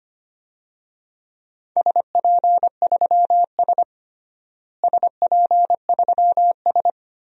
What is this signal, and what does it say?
Morse code, heard as a keyed tone, about 25 wpm, 700 Hz. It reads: SP3S SP3S
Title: DXCC-100 Call Signs - 4 Characters - Repeated 2x - 25wpm